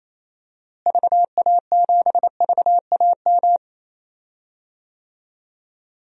Morse code, heard as a keyed tone, about 28 wpm, 700 Hz. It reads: VA7VAM
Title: DXCC-100 Call Signs - 6 Characters - Repeated 2x - 28wpm